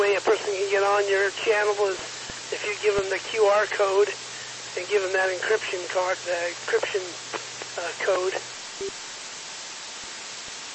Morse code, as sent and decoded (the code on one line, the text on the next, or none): none